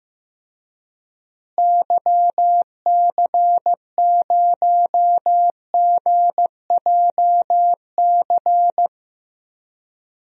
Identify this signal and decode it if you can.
Morse code, heard as a keyed tone, about 15 wpm, 700 Hz. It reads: YC0GJC
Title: DXCC-100 Call Signs - 6 Characters - Repeated 3x - 15wpm